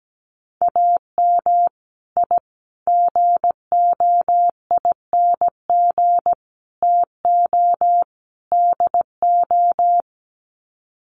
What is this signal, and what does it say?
Morse code, heard as a keyed tone, about 17 wpm, 700 Hz. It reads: AM I GOING TO DO
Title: Sets of 5 Words Encore 17wpm